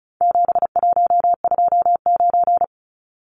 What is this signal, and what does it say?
Morse code, heard as a keyed tone, about 34 wpm, 700 Hz. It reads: 7129
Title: Numbers - 4-Digit 35wpm